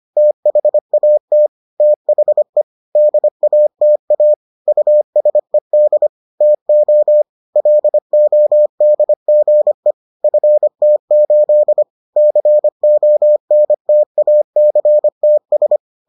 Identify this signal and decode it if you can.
Morse code, heard as a keyed tone, about 25 wpm, 600 Hz. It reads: THAT THE DATA USED TO LODGE FT8 CONTACTS